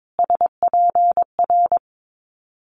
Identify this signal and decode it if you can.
Morse code, heard as a keyed tone, about 22 wpm, 700 Hz. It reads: SPR
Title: Multi-letter Phonemes Repeated 3x 22wpm